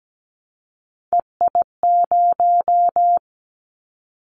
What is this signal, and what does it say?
Morse code, heard as a keyed tone, about 17 wpm, 700 Hz. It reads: EI0
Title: Callsign Prefixes for Top 50 DXCC - 17wpm